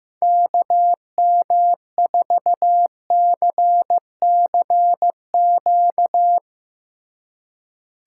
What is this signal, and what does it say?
Morse code, heard as a keyed tone, about 15 wpm, 700 Hz. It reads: KM4CCQ
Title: Speed Racing - USA 2x3 Call Signs 15wpm